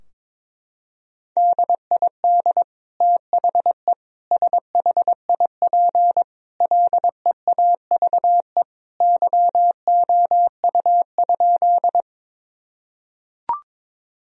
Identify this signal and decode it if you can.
Morse code, heard as a keyed tone, about 22 wpm, 700 Hz. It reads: DID THE SHIP LEAVE YOU?
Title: Sentences from Top 1000 Words - Review Part 2 - 22wpm